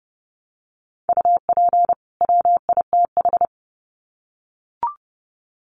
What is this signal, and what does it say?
Morse code, heard as a keyed tone, about 30 wpm, 700 Hz. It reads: UP WITH
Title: Sets of 2 Words 30wpm